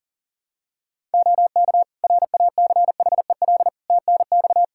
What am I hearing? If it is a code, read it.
Morse code, heard as a keyed tone, about 40 wpm, 700 Hz. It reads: OK RACHEL TNX
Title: K1USN SST QSO Practice 40wpm